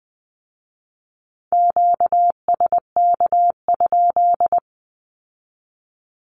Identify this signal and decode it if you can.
Morse code, heard as a keyed tone, about 20 wpm, 700 Hz. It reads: QSK?
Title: Q-Codes with Spoken Meaning - 20wpm